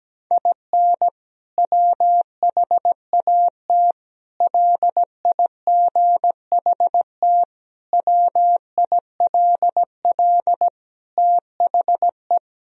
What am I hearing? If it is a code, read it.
Morse code, heard as a keyed tone, about 17 wpm, 700 Hz. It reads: IN WHAT LIGHT WILL THE
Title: Sentences from Top 1000 Words - Review Part 2 - 17wpm